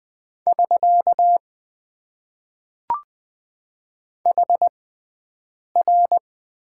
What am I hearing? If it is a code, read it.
Morse code, heard as a keyed tone, about 20 wpm, spaced as for 7 wpm, 700 Hz. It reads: <SK> HR
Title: Top 90 QSO Elements - 6x Character Spacing - 20wpm